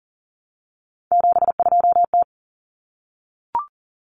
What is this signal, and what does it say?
Morse code, heard as a keyed tone, about 40 wpm, 700 Hz. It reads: 72T
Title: Common Cut Numbers - 3-Digit 40wpm